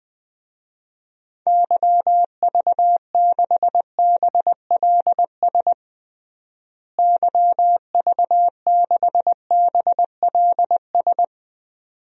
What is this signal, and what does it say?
Morse code, heard as a keyed tone, about 20 wpm, 700 Hz. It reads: YV6BLS YV6BLS
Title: DXCC-100 Call Signs - 6 Characters - Repeated 2x - 20wpm